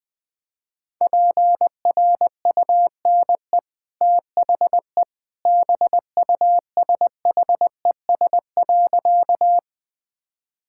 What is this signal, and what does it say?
Morse code, heard as a keyed tone, about 21 wpm, 700 Hz. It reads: PRUNE THE BUSHES.
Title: Easy Sentences with Hints - 20wpm